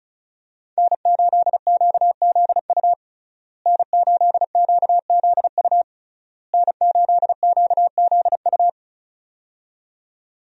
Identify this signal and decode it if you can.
Morse code, heard as a keyed tone, about 34 wpm, 700 Hz. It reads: N8QZU N8QZU N8QZU
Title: USA Call Signs - 5 Characters - Repeated 3x - 35wpm